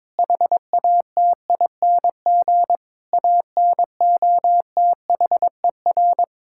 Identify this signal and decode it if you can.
Morse code, heard as a keyed tone, about 22 wpm, 700 Hz. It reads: HATING ANOTHER
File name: Sentences with 9-Word Phrases - 22wpm